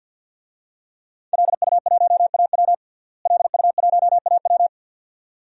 Morse code, heard as a keyed tone, about 49 wpm, 700 Hz. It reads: LU1AW LU1AW
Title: DXCC-100 Call Signs - 5 Characters - Repeated 2x - 50wpm